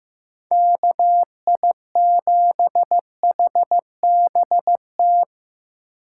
Morse code, heard as a keyed tone, about 15 wpm, 700 Hz. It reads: KI7HBT
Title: USA Call Signs - 6 Characters - 6x Character Spacing - 15wpm